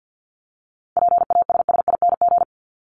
Morse code, heard as a keyed tone, about 50 wpm, 700 Hz. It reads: PASSING
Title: ING Suffix - Top 100 Words with Base Words 50wpm